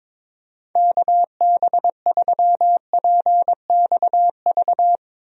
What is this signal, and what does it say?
Morse code, heard as a keyed tone, about 22 wpm, 700 Hz. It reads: KB3PXV
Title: USA Call Signs - 6 Characters - Repeated 3x - 22wpm